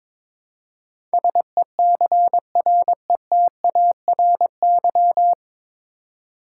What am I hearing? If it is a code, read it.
Morse code, heard as a keyed tone, about 22 wpm, 700 Hz. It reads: SECRETARY
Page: Morse Code Ninja practice 22wpm